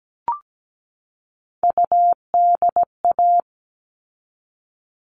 Morse code, heard as a keyed tone, about 17 wpm, 700 Hz. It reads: UDA
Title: All Cut Numbers - 3-Digit 17wpm